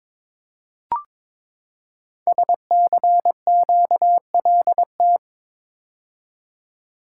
Morse code, heard as a keyed tone, about 22 wpm, 700 Hz. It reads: SCQLT
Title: Code Groups - 5 Letters - 22wpm